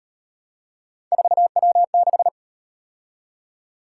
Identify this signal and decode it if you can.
Morse code, heard as a keyed tone, about 36 wpm, 700 Hz. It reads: 4W6